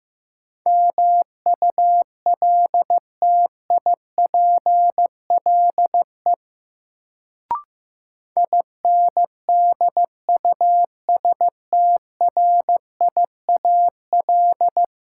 Morse code, heard as a keyed tone, about 15 wpm, 700 Hz. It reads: MULTIPLE INDUSTRIAL
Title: Sets of 5 Words - New Word Drill - 15wpm